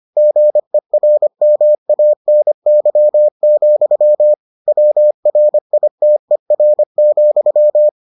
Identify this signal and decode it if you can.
Morse code, heard as a keyed tone, about 25 wpm, 600 Hz. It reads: GERMANY, WRITER,